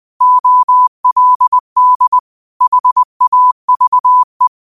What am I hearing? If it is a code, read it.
Morse code, heard as a keyed tone, about 20 wpm, 1000 Hz. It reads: OLD HAVE